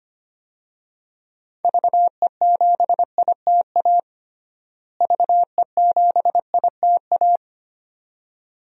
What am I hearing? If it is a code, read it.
Morse code, heard as a keyed tone, about 25 wpm, 700 Hz. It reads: VE7ITA VE7ITA
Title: DXCC-100 Call Signs - 6 Characters - Repeated 2x - 25wpm